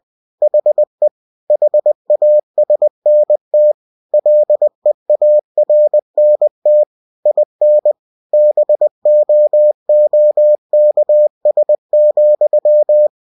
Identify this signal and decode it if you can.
Morse code, heard as a keyed tone, about 20 wpm, 600 Hz. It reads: HE HASNT LEARNT IN BOOKS,
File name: cw-books / winnie-the-pooh / winnie-20wpm-page-0091